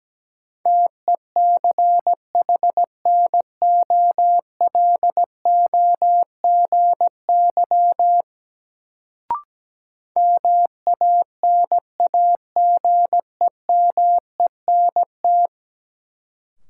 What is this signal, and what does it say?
Morse code, heard as a keyed tone, about 17 wpm, 700 Hz. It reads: TECHNOLOGY MANAGEMENT
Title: Sentences from Top 1200 Words - New Word Drill Alt - 17wpm